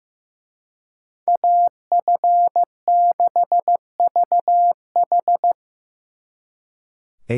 Morse code, heard as a keyed tone, about 18 wpm, 700 Hz. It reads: AF6VH
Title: Speed Racing - USA 2x2 Call Signs 15wpm